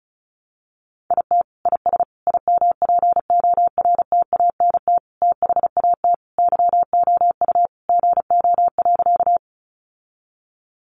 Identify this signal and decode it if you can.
Morse code, heard as a keyed tone, about 34 wpm, 700 Hz. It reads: IT IS IMPORTANT THAT YOU GO.
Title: Sentences from Top 600 Words 35wpm